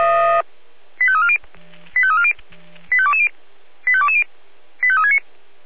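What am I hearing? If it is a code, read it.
Five-tone selective call (ZVEI), heard as quick runs of five tones, tones of about 70 ms. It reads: 85340 85340 85220 85220 85369
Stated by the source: ZVEI five-tone selective call received on SDR.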